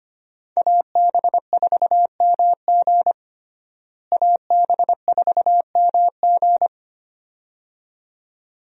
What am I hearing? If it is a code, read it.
Morse code, heard as a keyed tone, about 25 wpm, 700 Hz. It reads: AB4MG AB4MG
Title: USA Call Signs - 5 Characters - Repeated 2x - 25wpm